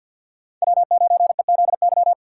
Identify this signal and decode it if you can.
Morse code, heard as a keyed tone, about 50 wpm, 700 Hz. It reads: W9EZY
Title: USA Call Signs - 5 Characters - 2x Character Spacing - 50wpm